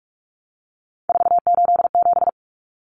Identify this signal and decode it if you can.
Morse code, heard as a keyed tone, about 44 wpm, 700 Hz. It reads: 487
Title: Numbers - 3-Digit 45wpm